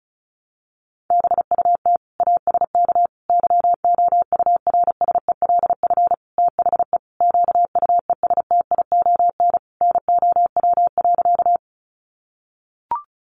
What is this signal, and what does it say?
Morse code, heard as a keyed tone, about 34 wpm, 700 Hz. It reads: BUT ASK YOURSELF THE QUESTION NOW.
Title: Sentences from Top 600 Words 35wpm